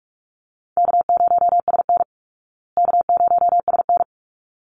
Morse code, heard as a keyed tone, about 44 wpm, 700 Hz. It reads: K0SN K0SN
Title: Call Signs Repeated 3x - 45wpm